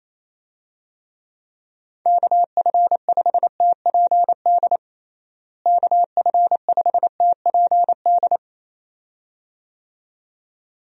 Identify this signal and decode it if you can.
Morse code, heard as a keyed tone, about 28 wpm, 700 Hz. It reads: KF5TPD KF5TPD
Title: USA Call Signs - 6 Characters - Repeated 2x - 28wpm